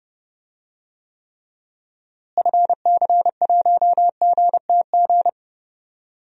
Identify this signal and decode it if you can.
Morse code, heard as a keyed tone, about 30 wpm, 700 Hz. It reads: FC1GTG
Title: DXCC-100 Call Signs - 6 Characters - Repeated 3x - 30wpm